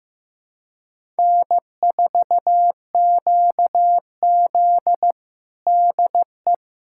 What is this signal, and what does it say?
Morse code, heard as a keyed tone, about 15 wpm, 700 Hz. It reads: N4QZ DE
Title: USA Callsign DE Callsign - 4 Characters - 15wpm